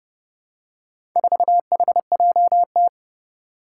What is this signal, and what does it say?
Morse code, heard as a keyed tone, about 30 wpm, 700 Hz. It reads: 4HJT
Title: Four Challenging Letter-Number Combinations 30wpm